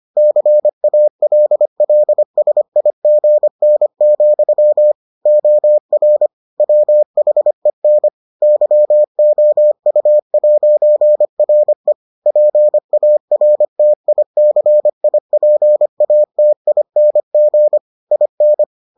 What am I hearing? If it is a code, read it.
Morse code, heard as a keyed tone, about 25 wpm, 600 Hz. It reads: CALLSIGN, OR WHEN YOU'RE PARTICIPATING IN